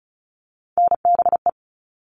Morse code, heard as a keyed tone, about 34 wpm, 700 Hz. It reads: NBE
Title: All Cut Numbers - 3-Digit 35wpm